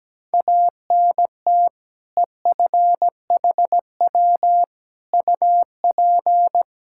Morse code, heard as a keyed tone, about 17 wpm, 700 Hz. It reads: ANT EFHW UP
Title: Practice QSO - Exchange 2 - Info - 17wpm